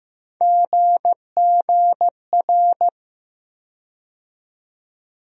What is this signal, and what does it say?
Morse code, heard as a keyed tone, about 15 wpm, 700 Hz. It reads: GGR
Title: USA Call Sign Postfixes - 3 Letters 15wpm